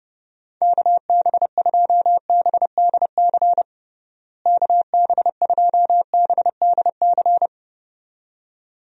Morse code, heard as a keyed tone, about 31 wpm, 700 Hz. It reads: KB2BDC KB2BDC
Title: USA Call Signs - 6 Characters - Repeated 2x - 30wpm